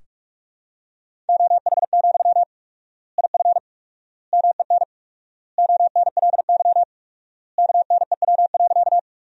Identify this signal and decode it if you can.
Morse code, heard as a keyed tone, about 46 wpm, 700 Hz. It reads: OH, IF MEN ONLY KNEW.
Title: 4x Word-Spacing - Sentences from Top 500 Words - 45wpm